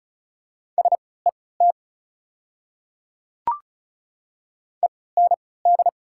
Morse code, heard as a keyed tone, about 34 wpm, spaced as for 20 wpm, 690 Hz. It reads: SET END
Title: 3x Character Spacing - Top 300 Words - 35wpm